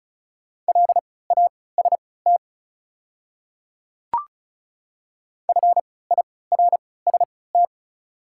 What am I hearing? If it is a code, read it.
Morse code, heard as a keyed tone, about 36 wpm, spaced as for 20 wpm, 700 Hz. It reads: LAST FIRST
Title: 3x Character Spacing - Top 300 Words - 35wpm